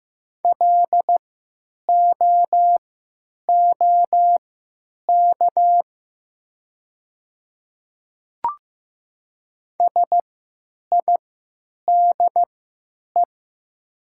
Morse code, heard as a keyed tone, about 15 wpm, spaced as for 8 wpm, 700 Hz. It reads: LOOK SIDE
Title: 3x Character Spacing - Top 200 Words - 15wpm